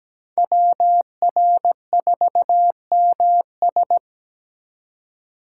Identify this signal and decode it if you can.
Morse code, heard as a keyed tone, about 17 wpm, 700 Hz. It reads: WR4MS
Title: USA Call Signs - 2x Character Spacing - 17wpm